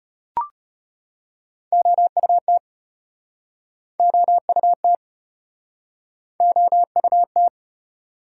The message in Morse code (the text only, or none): OUT OUT OUT